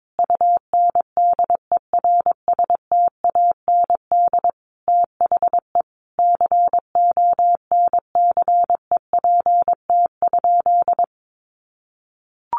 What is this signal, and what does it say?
Morse code, heard as a keyed tone, about 22 wpm, 700 Hz. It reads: UNDERSTAND THE CONCEPT?
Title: Sentences from Top 1100 Words 22wpm